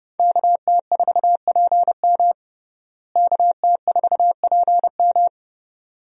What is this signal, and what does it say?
Morse code, heard as a keyed tone, about 30 wpm, 700 Hz. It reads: KT4PM KT4PM
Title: USA Call Signs - 5 Characters - Repeated 2x - 30wpm